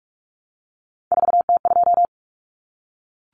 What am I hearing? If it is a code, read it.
Morse code, heard as a keyed tone, about 44 wpm, 700 Hz. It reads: 4T2
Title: Common Cut Numbers - 3-Digit 45wpm